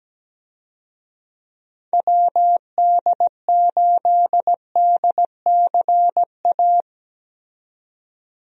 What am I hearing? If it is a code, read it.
Morse code, heard as a keyed tone, about 17 wpm, 700 Hz. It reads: WD8DCA
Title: USA Call Signs - 6 Characters - 17wpm